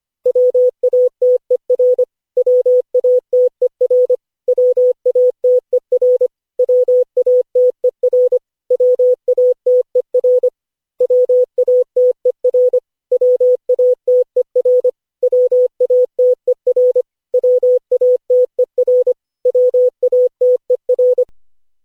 Morse code, heard as a keyed tone, about 25 wpm, 500 Hz. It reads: WATER WATER WATER WATER WATER WATER WATER WATER WATER WATER
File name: cw / Session5 cw training